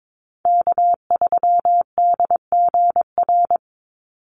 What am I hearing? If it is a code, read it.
Morse code, heard as a keyed tone, about 22 wpm, 690 Hz. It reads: K3DGR
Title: Call Signs Repeated 2x - 22wpm